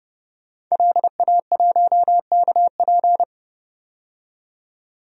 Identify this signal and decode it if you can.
Morse code, heard as a keyed tone, about 30 wpm, 700 Hz. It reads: LA1KP
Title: DXCC-100 Call Signs - 5 Characters - 30wpm